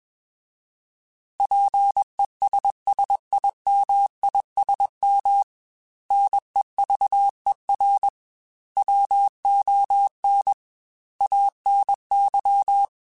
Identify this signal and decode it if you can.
Morse code, heard as a keyed tone, about 21 wpm, 800 Hz. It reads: PESSIMISM NEVER WON ANY